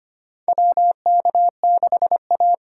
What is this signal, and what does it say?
Morse code, heard as a keyed tone, about 25 wpm, 700 Hz. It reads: WK6A
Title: USA Call Signs - 6x Character Spacing - 25wpm